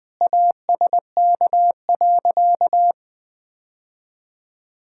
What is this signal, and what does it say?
Morse code, heard as a keyed tone, about 21 wpm, 700 Hz. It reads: ASK.
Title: Sentences with 3-Word Phrases - 20wpm